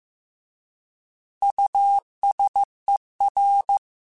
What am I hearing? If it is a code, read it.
Morse code, heard as a keyed tone, about 15 wpm, 800 Hz. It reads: USER
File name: Alphabetical / Alphabetical mp3s / Us @13